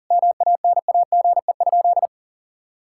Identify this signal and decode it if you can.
Morse code, heard as a keyed tone, about 40 wpm, 700 Hz. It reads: MANAGE?
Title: Sentences from Top 1600 Words 40wpm